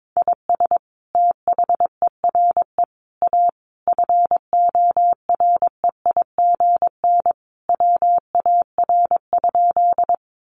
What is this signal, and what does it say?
Morse code, heard as a keyed tone, about 22 wpm, 700 Hz. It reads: IS THERE A FOREIGN WAR?
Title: Sentences from Top 1400 Words 22wpm